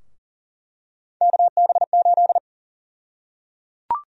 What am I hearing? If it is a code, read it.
Morse code, heard as a keyed tone, about 37 wpm, 700 Hz. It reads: KB8
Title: USA Call Sign Prefixes - 40wpm